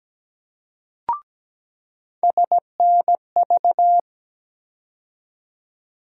Morse code, heard as a keyed tone, about 17 wpm, 700 Hz. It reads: SNV
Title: USA Call Sign Postfixes - 3 Letters 17wpm